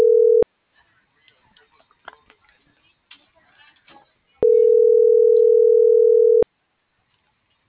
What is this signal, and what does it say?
Ringback tone (North America), 440 + 480 Hz.